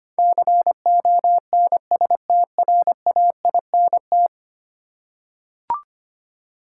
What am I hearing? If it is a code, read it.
Morse code, heard as a keyed tone, about 25 wpm, 700 Hz. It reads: CONSTRAINT
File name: Ten Letter Words - 25wpm